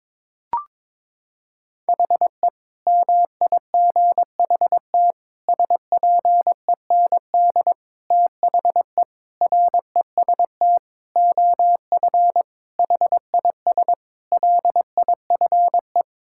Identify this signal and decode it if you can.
Morse code, heard as a keyed tone, about 22 wpm, 700 Hz. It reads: HE MIGHT SPEND THE REST OF HIS LIFE
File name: practice / 22wpm / Sentences with 8-Word Phrases - 22wpm